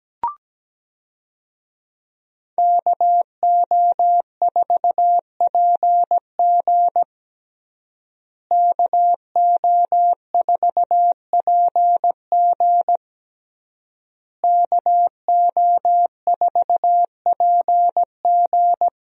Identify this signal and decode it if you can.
Morse code, heard as a keyed tone, about 17 wpm, 700 Hz. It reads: KO4PG KO4PG KO4PG
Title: USA Call Signs - 5 Characters - Repeated 3x - 17wpm